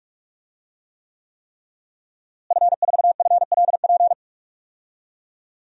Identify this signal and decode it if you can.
Morse code, heard as a keyed tone, about 48 wpm, 700 Hz. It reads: F4FLP
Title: DXCC-100 Call Signs - 5 Characters - Repeated 3x - 45wpm